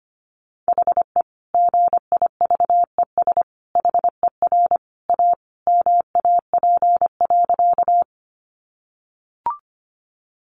Morse code, heard as a keyed tone, about 25 wpm, 700 Hz. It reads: HE GIVES HER A MAP.
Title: Additional Easy Phrases 25wpm